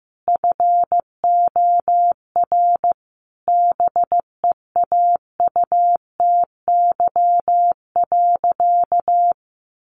Morse code, heard as a keyed tone, about 15 wpm, 700 Hz. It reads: FOR BEAUTY.
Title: Sentences from Top 1500 Words 15wpm